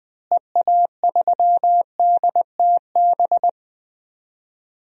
Morse code, heard as a keyed tone, about 21 wpm, 700 Hz. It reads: EA3DTB
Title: DXCC-100 Call Signs - 6 Characters - Repeated 2x - 20wpm